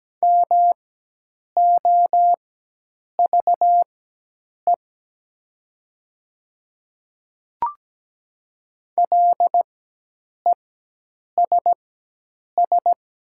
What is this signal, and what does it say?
Morse code, heard as a keyed tone, about 17 wpm, spaced as for 8 wpm, 700 Hz. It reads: MOVE LESS
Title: 4x Character Spacing - Top 500 Words - 17wpm